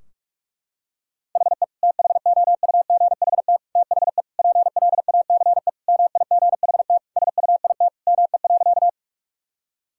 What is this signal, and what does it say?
Morse code, heard as a keyed tone, about 46 wpm, 700 Hz. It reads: HE THOUGHT THE PLACE MIGHT SUIT ME.
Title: Sentences from Top 1600 Words 45wpm